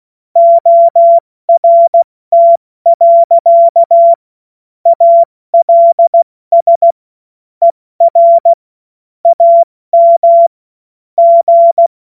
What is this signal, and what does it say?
Morse code, heard as a keyed tone, about 16 wpm, 680 Hz. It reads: ORT. ALS ER AM G
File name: Morsekurs / Witze / Witz 20. September 2022 70 F80